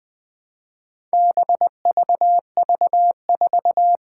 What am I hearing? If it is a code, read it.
Morse code, heard as a keyed tone, about 20 wpm, 700 Hz. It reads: BVV4
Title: Four Challenging Letter-Number Combinations 20wpm